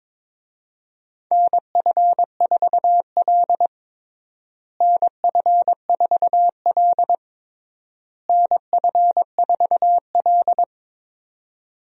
Morse code, heard as a keyed tone, about 22 wpm, 700 Hz. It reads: NF4L NF4L NF4L
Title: USA Call Signs - 4 Characters - Repeated 3x - 22wpm